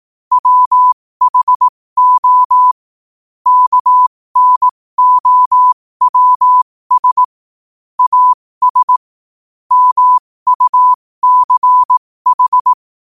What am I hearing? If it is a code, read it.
Morse code, heard as a keyed tone, about 18 wpm, spaced as for 15 wpm, 1000 Hz. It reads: WHO KNOWS AS MUCH
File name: qotdfiles / 15wpm